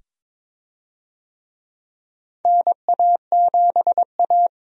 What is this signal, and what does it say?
Morse code, heard as a keyed tone, about 22 wpm, 700 Hz. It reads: NA7A
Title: USA Call Signs - 4 Characters - 22wpm